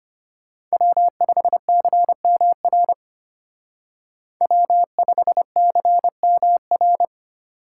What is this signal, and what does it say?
Morse code, heard as a keyed tone, about 33 wpm, 700 Hz. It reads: W5CMR W5CMR